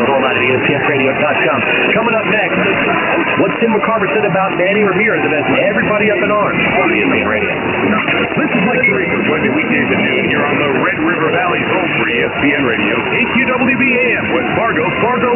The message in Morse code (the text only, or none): none